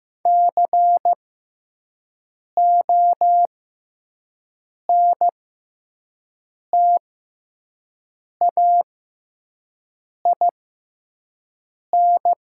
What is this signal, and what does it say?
Morse code, heard as a keyed tone, about 15 wpm, spaced as for 5 wpm, 700 Hz. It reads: CONTAIN
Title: 6x Character Spacing - Top 500 Words - 15wpm